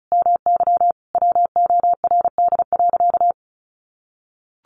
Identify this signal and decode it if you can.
Morse code, heard as a keyed tone, about 34 wpm, 700 Hz. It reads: MY WORD.
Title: Sentences from Top 1100 Words 35wpm